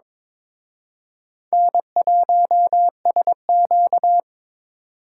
Morse code, heard as a keyed tone, about 26 wpm, 700 Hz. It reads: N1SQ
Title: Speed Racing - USA 1x2 Call Signs 22wpm